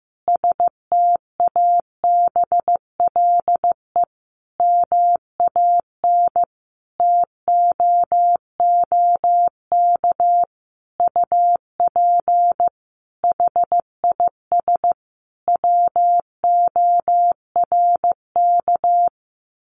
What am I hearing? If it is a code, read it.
Morse code, heard as a keyed tone, about 14 wpm, 700 Hz. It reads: STABLE MAN TOOK UP HIS WORK
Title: Sentences from Top 1500 Words 15wpm